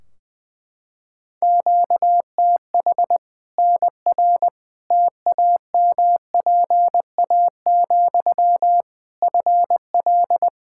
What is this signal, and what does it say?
Morse code, heard as a keyed tone, about 20 wpm, 700 Hz. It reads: QTH NR TAMPA, FL
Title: Practice QSO - Exchange 1 - Beginning - 20wpm